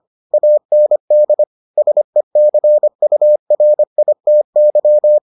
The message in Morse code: AND SECURITY

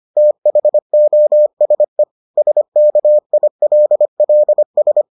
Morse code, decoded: THOSE SKILLS

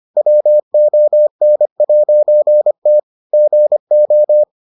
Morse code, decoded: WON'T GO

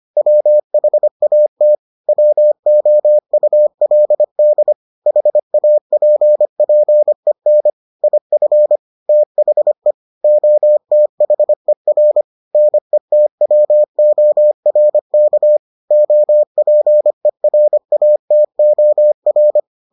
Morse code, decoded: WHAT WOULD HAPPEN IF THE OTHER NETWORK OPERATOR

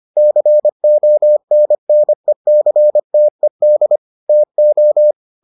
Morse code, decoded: CONNECTED TO

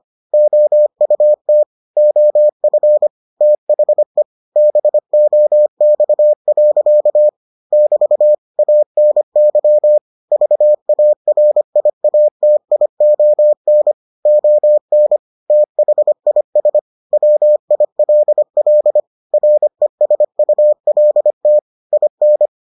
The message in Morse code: OUT OF THE BOX. <BT> ANY VARIATION ON THIS WILL RESULT IN

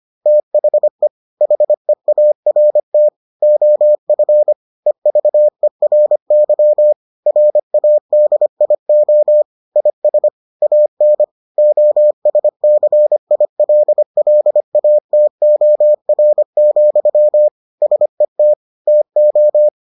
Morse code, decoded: THE HEART OF EVERY RADIO IS AN OSCILLATOR, SET TO

